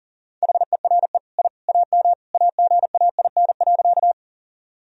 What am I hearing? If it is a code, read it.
Morse code, heard as a keyed tone, about 37 wpm, 700 Hz. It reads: HERE I AM AGAIN.
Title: Sentences from Top 1000 Words - Review Part 2 - 40wpm